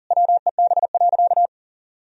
Morse code, heard as a keyed tone, about 37 wpm, 700 Hz. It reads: WEB.
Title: Sentences from Top 1000 Words - Review Part 4 - 40wpm